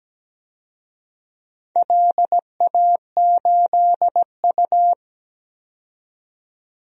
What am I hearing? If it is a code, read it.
Morse code, heard as a keyed tone, about 17 wpm, 700 Hz. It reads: LA8U
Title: DXCC-100 Call Signs - 4 Characters - 17wpm